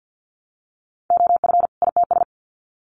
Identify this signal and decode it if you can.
Morse code, heard as a keyed tone, about 50 wpm, 700 Hz. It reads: OF ITS
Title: Sets of 2 Words 50wpm